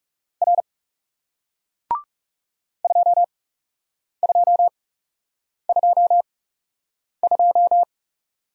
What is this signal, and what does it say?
Morse code, heard as a keyed tone, about 39 wpm, spaced as for 8 wpm, 700 Hz. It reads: R 2222